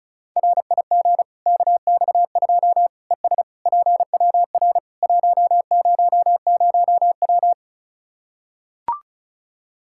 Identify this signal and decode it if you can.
Morse code, heard as a keyed tone, about 34 wpm, 700 Hz. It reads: RIG KX2 ES PWR 100W
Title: K1USN SST QSO Practice 35wpm